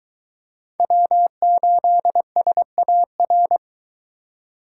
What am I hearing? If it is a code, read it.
Morse code, heard as a keyed tone, about 22 wpm, 700 Hz. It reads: W8SAR